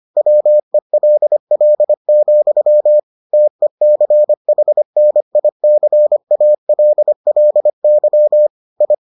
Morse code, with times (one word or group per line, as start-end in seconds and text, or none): 0.2-3.0 WELL,
3.3-8.5 TECHNICALLY
8.8-8.9 I